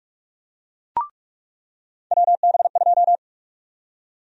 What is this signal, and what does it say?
Morse code, heard as a keyed tone, about 45 wpm, 700 Hz. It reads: WB2
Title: USA Call Sign Prefixes - 45wpm